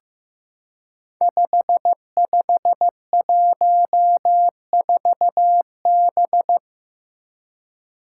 Morse code, heard as a keyed tone, about 15 wpm, 700 Hz. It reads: 5514B
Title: Five Challenging Letter-Number Combinations 15wpm